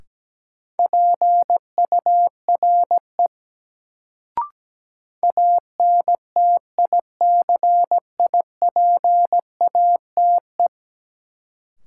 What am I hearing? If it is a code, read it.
Morse code, heard as a keyed tone, about 17 wpm, 700 Hz. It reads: PURE ANTICIPATE
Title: Sentences from Top 1500 Words - New Word Drill Alt - 17wpm